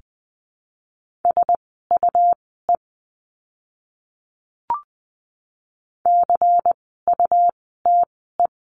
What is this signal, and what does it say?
Morse code, heard as a keyed tone, about 20 wpm, spaced as for 14 wpm, 700 Hz. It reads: SUE CUTE
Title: 101 Lesson 17f - Letter u - Words 20wpm